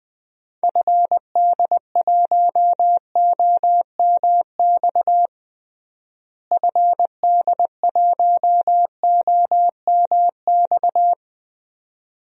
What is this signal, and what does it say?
Morse code, heard as a keyed tone, about 21 wpm, 700 Hz. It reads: FD1OMX FD1OMX
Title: DXCC-100 Call Signs - 6 Characters - Repeated 2x - 20wpm